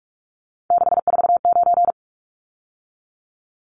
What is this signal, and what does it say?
Morse code, heard as a keyed tone, about 44 wpm, 700 Hz. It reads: 649
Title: Numbers - 3-Digit 45wpm